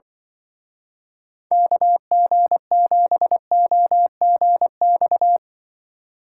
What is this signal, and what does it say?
Morse code, heard as a keyed tone, about 24 wpm, 700 Hz. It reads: KG7OGX
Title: Speed Racing - USA 2x3 Call Signs 20wpm